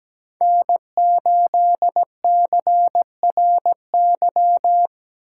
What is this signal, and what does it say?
Morse code, heard as a keyed tone, about 17 wpm, 700 Hz. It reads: N 8 C R Y
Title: USA Call Signs - 8x Character Spacing - 17wpm